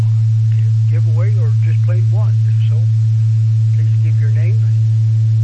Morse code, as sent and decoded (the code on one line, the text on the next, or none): none